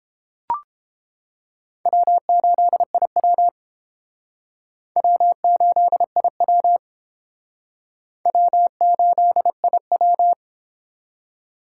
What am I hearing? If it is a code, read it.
Morse code, heard as a keyed tone, about 30 wpm, 700 Hz. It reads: W8IW W8IW W8IW